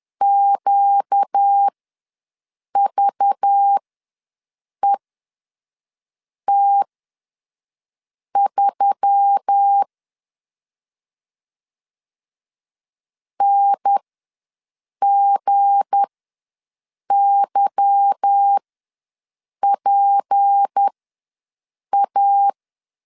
Morse code, heard as a keyed tone, about 10 wpm, spaced as for 5 wpm, 800 Hz. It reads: QVET3 NGYPA
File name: LEZIONI / 8 YJ45 / 8 YJ ws 10wpm 4043